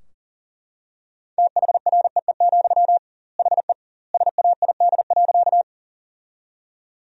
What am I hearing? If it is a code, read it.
Morse code, heard as a keyed tone, about 40 wpm, 700 Hz. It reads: THREE, HE SAID.
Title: 2x Word-Spacing - Sentences from Top 200 Words - 40wpm